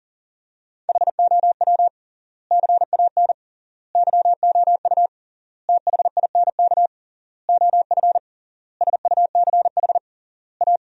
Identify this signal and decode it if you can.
Morse code, heard as a keyed tone, about 40 wpm, 700 Hz. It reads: HOW CAN YOU THINK OF SUCH A
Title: 3x Word-Spacing - Sentences from Top 200 Words - 40wpm